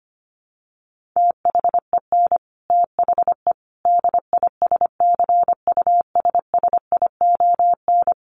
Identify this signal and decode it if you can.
Morse code, heard as a keyed tone, about 25 wpm, 700 Hz. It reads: THEN THE DISCUSSION
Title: Sentences from Top 1100 Words 25wpm